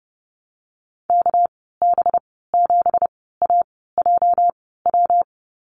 Morse code, heard as a keyed tone, about 30 wpm, spaced as for 17 wpm, 700 Hz. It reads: KB7AJW
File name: Lesson 37g - Letter x - Call Signs 30wpm